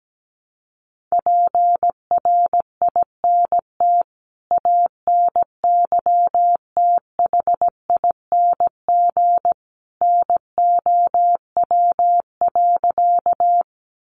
Morse code, heard as a keyed tone, about 17 wpm, 700 Hz. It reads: PRINT ANYTHING NOW.
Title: Sentences from Top 1400 Words 17wpm